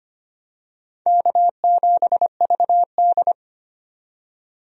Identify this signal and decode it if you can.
Morse code, heard as a keyed tone, about 25 wpm, 700 Hz. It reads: K7VD
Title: USA Call Signs - 4 Characters - 25wpm